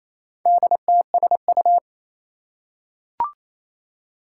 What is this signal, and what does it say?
Morse code, heard as a keyed tone, about 29 wpm, 700 Hz. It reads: DTSU
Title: Four Challenging Letter-Number Combinations 28wpm